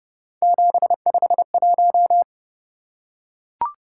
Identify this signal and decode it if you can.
Morse code, heard as a keyed tone, about 29 wpm, 700 Hz. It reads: 751